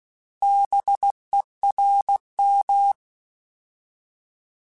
Morse code, heard as a keyed tone, about 16 wpm, 800 Hz. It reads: BERM
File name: Alphabetical / Alphabetical mp3s / Bs @15